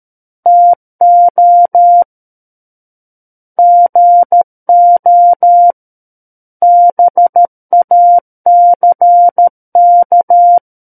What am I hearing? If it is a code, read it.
Morse code, heard as a keyed tone, about 13 wpm, 700 Hz. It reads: TO GO BACK
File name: Book One / 04-Chapter 4 - 13 WPM